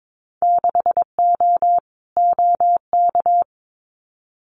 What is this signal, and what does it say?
Morse code, heard as a keyed tone, about 22 wpm, 700 Hz. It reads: 6O OK
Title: Winter Field Day Exchanges Encore 22wpm